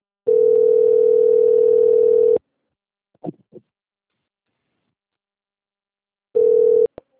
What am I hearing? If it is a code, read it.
Ringback tone (North America), 440 + 480 Hz.